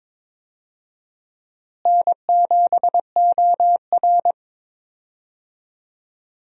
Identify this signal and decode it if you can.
Morse code, heard as a keyed tone, about 22 wpm, 700 Hz. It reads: N7OR
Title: USA Call Signs - 4 Characters - Repeated 3x - 22wpm